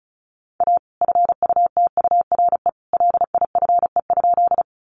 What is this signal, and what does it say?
Morse code, heard as a keyed tone, about 34 wpm, 700 Hz. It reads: A FUTURE LIFE?
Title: Sentences from Top 700 Words 35wpm